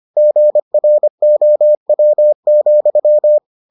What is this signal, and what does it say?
Morse code, heard as a keyed tone, about 25 wpm, 600 Hz. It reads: GROW,